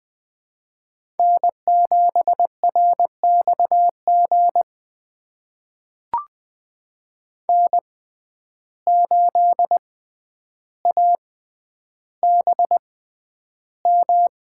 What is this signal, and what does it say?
Morse code, heard as a keyed tone, about 20 wpm, spaced as for 10 wpm, 700 Hz. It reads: N7RXG N8ABM
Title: USA Call Signs - 5 Characters - 6x Character Spacing - 20wpm